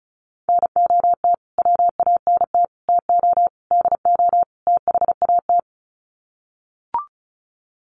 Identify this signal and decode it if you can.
Morse code, heard as a keyed tone, about 34 wpm, 700 Hz. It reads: NOT WANT TO DO THAT
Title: Sets of 5 Words Encore 35wpm